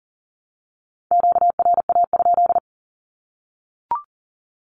Morse code, heard as a keyed tone, about 40 wpm, 700 Hz. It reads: QRA?
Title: Q-Codes Repeated 3x 40wpm